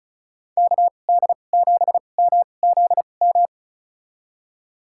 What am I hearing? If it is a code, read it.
Morse code, heard as a keyed tone, about 34 wpm, spaced as for 25 wpm, 700 Hz. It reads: KD7MZM
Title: USA Call Signs - 6 Characters - 2x Character Spacing - 35wpm